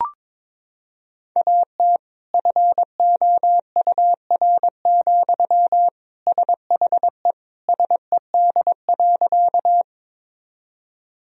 Morse code, heard as a keyed tone, about 21 wpm, 700 Hz. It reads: AT FOUR, SHE SED.
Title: Sentences from Top 300 Words and Abbreviations 22wpm